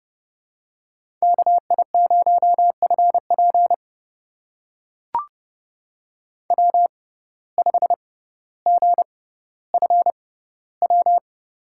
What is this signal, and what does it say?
Morse code, heard as a keyed tone, about 29 wpm, spaced as for 15 wpm, 700 Hz. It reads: KI0FP W5GFW
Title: USA Call Signs - 5 Characters - 6x Character Spacing - 30wpm